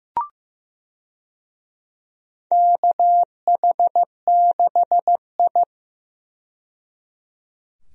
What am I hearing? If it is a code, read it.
Morse code, heard as a keyed tone, about 15 wpm, 700 Hz. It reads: KH6I
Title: DXCC-100 Call Signs - 4 Characters - 15wpm